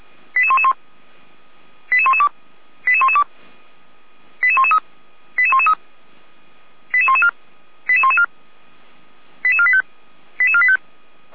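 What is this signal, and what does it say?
Five-tone selective call (ZVEI), heard as quick runs of five tones, tones of about 70 ms. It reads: 88111 88112 88112 88113 88113 88114 88114 88485 88485
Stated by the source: ZVEI five-tone selective call received on SDR, from feuerwehr-orlamuende fire brigade server.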